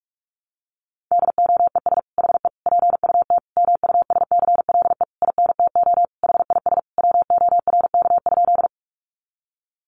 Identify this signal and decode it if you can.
Morse code, heard as a keyed tone, about 44 wpm, 700 Hz. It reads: DOES HE PUT MUSCLE INTO HIS WORK?
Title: Sentences from Top 1300 Words 45wpm